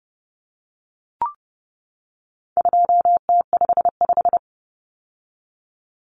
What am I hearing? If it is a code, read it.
Morse code, heard as a keyed tone, about 30 wpm, 700 Hz. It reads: 2T55
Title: Common Cut Numbers - 4-Digit 30wpm